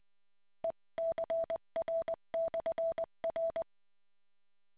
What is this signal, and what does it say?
Morse code, heard as a keyed tone, about 20 wpm, 660 Hz. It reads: ECR/R